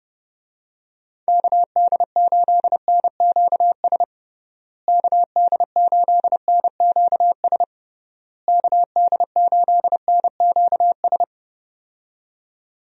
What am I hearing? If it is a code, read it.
Morse code, heard as a keyed tone, about 32 wpm, 700 Hz. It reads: KD8NQS KD8NQS KD8NQS
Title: USA Call Signs - 6 Characters - Repeated 3x - 30wpm